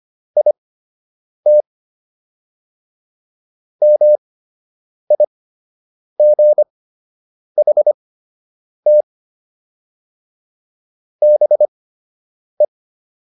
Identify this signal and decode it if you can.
Morse code, heard as a keyed tone, about 25 wpm, spaced as for 8 wpm, 600 Hz. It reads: IT MIGHT BE